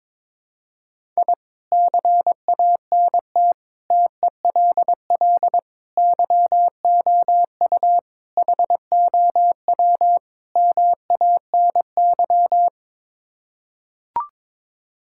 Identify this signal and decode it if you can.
Morse code, heard as a keyed tone, about 22 wpm, 700 Hz. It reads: I CANT TELL YOU HOW MANY
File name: practice / 22wpm / Sets of 6 Words - 22wpm